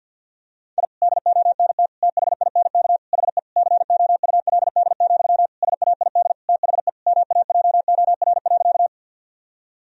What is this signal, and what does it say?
Morse code, heard as a keyed tone, about 52 wpm, 700 Hz. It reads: I DONT THINK HE COULD, SAID THE MAJOR.
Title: Sentences from Top 1000 Words - Review Part 4 - 50wpm